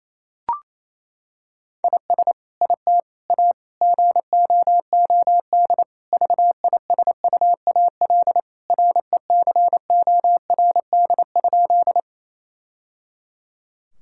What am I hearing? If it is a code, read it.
Morse code, heard as a keyed tone, about 28 wpm, 700 Hz. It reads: IS IT A GOOD VISUAL RECORD?
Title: Sentences from Top 1000 Words - Review Part 6 - 28wpm